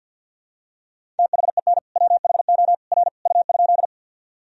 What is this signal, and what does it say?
Morse code, heard as a keyed tone, about 50 wpm, 700 Hz. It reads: THEN WHO R U?
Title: Sentences from Top 100 Words and Abbreviations 50wpm